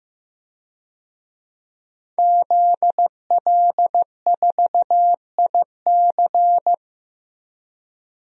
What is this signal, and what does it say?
Morse code, heard as a keyed tone, about 15 wpm, 700 Hz. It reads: ZL4IC
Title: DXCC-100 Call Signs - 5 Characters - 15wpm